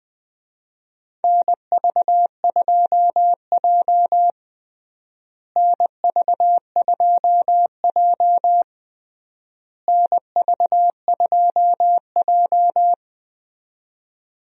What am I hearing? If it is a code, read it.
Morse code, heard as a keyed tone, about 21 wpm, 700 Hz. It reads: NV2J NV2J NV2J
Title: USA Call Signs - 4 Characters - Repeated 3x - 20wpm